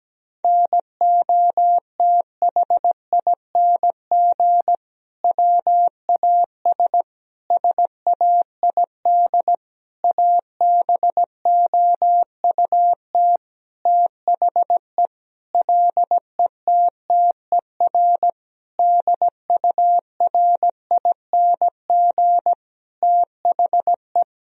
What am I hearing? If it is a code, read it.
Morse code, heard as a keyed tone, about 17 wpm, 700 Hz. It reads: NOTHING WAS SAID ABOUT THE LETTER DURING THE
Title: Sentences from Top 1000 Words - Review Part 2 - 17wpm